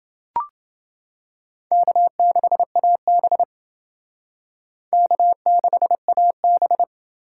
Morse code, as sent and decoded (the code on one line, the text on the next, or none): -.- -.... .- -... / -.- -.... .- -...
K6AB K6AB